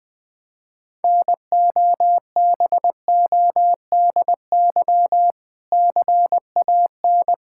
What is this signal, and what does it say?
Morse code, heard as a keyed tone, about 20 wpm, 700 Hz. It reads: NOBODY CAN
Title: Sentences from Top 1600 Words 20wpm